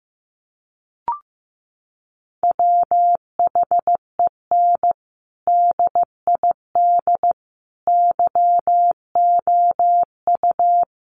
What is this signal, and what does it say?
Morse code, heard as a keyed tone, about 15 wpm, 700 Hz. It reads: WHEN DID YOU
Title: Sentences from Top 300 Words 15wpm